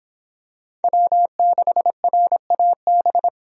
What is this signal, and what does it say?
Morse code, heard as a keyed tone, about 26 wpm, 700 Hz. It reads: W6RAB